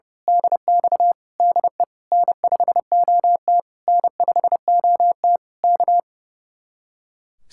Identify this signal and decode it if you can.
Morse code, heard as a keyed tone, about 28 wpm, 700 Hz. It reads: DX DE N5OT N5OT K
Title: Speed-Racing Common QSO Phrases Encore - 30wpm